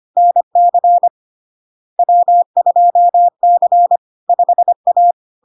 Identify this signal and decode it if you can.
Morse code, heard as a keyed tone, about 25 wpm, 700 Hz. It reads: NC W2C 5A
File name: Log-in-CW-25wpm-700-Hz